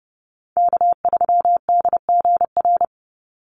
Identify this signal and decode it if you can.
Morse code, heard as a keyed tone, about 30 wpm, 700 Hz. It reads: K3DGR
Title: Call Signs Repeated 2x - 30wpm